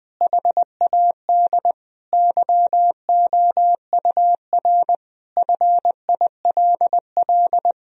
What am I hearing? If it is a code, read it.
Morse code, heard as a keyed tone, about 20 wpm, 700 Hz. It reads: HAD YOUR FILL
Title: Sentences from Top 1000 Words - Review Part 3 - 20wpm